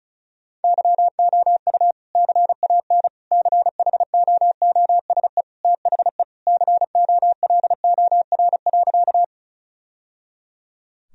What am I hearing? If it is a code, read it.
Morse code, heard as a keyed tone, about 34 wpm, 700 Hz. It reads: YOU CAN CHOOSE THE COLOR.
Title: Sentences from Top 1000 Words - Review Part 4 - 35wpm